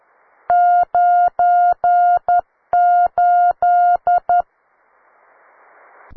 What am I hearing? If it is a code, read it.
Morse code, heard as a keyed tone, about 13 wpm, 710 Hz. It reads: 98
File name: wae / cw beginner / ua9cdv cw